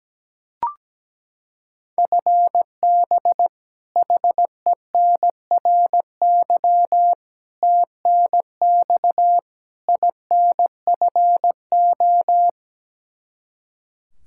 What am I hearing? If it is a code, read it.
Morse code, heard as a keyed tone, about 17 wpm, 700 Hz. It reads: FB HENRY TNX INFO
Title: Practice QSO - Exchange 3 - More Info - 17wpm